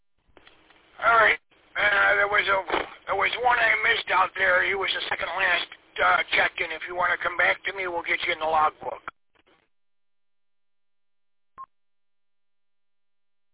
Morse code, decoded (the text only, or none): none